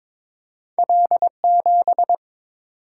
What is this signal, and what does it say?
Morse code, heard as a keyed tone, about 22 wpm, 700 Hz. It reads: L7
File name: practice / 22wpm / Two Letter-Number Combinations - 22wpm